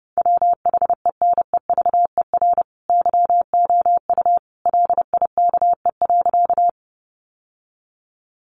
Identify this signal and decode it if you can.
Morse code, heard as a keyed tone, about 30 wpm, 700 Hz. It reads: WHENEVER YOU LIKE.
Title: Sentences from Top 1200 Words 30wpm